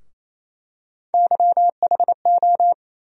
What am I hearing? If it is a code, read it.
Morse code, heard as a keyed tone, about 27 wpm, 700 Hz. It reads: YHO
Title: USA Call Sign Postfixes - 3 Letters 28wpm